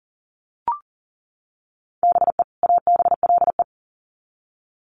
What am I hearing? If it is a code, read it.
Morse code, heard as a keyed tone, about 40 wpm, 700 Hz. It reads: BE ABLE
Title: Sets of 2 Words 40wpm